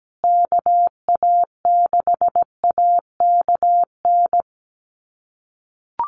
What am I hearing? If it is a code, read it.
Morse code, heard as a keyed tone, about 17 wpm, 700 Hz. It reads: KA6AKN
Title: USA Call Signs Repeated 3x 17wpm